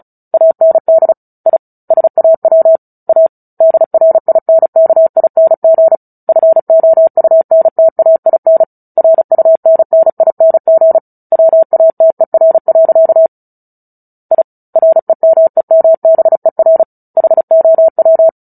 Morse code, heard as a keyed tone, about 34 wpm, 650 Hz. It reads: AND I SAW A DRINKING FOUNTAIN RUNNING WATER. I REMEMBER HOW
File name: Book Two / 09-Chapter 9 - 35 WPM